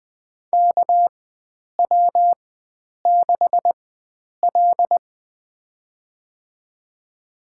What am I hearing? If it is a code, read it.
Morse code, heard as a keyed tone, about 20 wpm, spaced as for 9 wpm, 700 Hz. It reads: KW6L
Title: USA Call Signs - 4 Characters - 4x Character Spacing - 20wpm